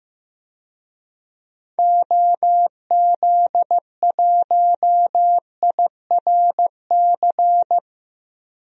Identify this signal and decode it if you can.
Morse code, heard as a keyed tone, about 15 wpm, 700 Hz. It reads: OZ1IRC
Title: DXCC-100 Call Signs - 6 Characters - 15wpm